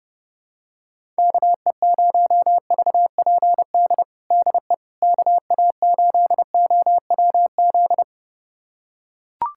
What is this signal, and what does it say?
Morse code, heard as a keyed tone, about 30 wpm, 700 Hz. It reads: KE0VPD DE KA8OWZ
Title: USA Callsign DE Callsign - 6 Characters - 30wpm